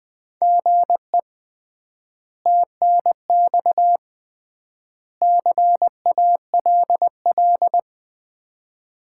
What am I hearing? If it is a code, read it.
Morse code, heard as a keyed tone, about 20 wpm, 700 Hz. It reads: GE TNX CALL
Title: Common QSO Phrases - 3x Word-Spacing - 20wpm